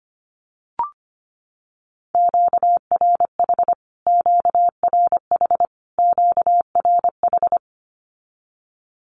Morse code, heard as a keyed tone, about 25 wpm, 700 Hz. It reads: QRH QRH QRH
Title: Q-Codes Repeated 3x 25wpm